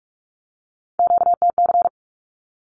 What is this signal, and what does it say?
Morse code, heard as a keyed tone, about 48 wpm, 700 Hz. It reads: QTC
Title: Q-Codes 45wpm